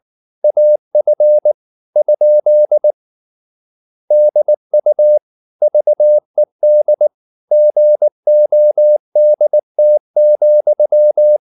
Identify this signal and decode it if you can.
Morse code, heard as a keyed tone, about 19 wpm, 600 Hz. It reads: AF? DU VED GODT,